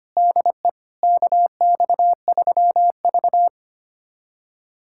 Morse code, heard as a keyed tone, about 25 wpm, 700 Hz. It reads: DE KX3V
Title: USA Callsign DE Callsign - 4 Characters - 25wpm